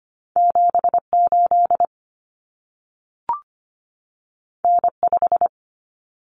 Morse code, heard as a keyed tone, about 25 wpm, 700 Hz. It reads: 78 N5
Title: Common Cut Numbers - 2-Digit 25wpm